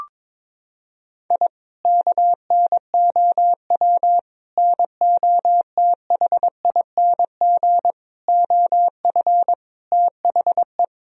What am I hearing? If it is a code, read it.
Morse code, heard as a keyed tone, about 22 wpm, 700 Hz. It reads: I KNOW NOTHING OF THE
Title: Sentences from Top 1000 Words - Review Part 4 - 22wpm